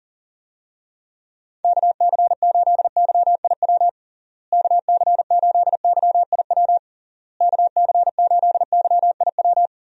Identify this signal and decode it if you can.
Morse code, heard as a keyed tone, about 40 wpm, 700 Hz. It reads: KC8YIW KC8YIW KC8YIW
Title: USA Call Signs - 6 Characters - Repeated 3x - 40wpm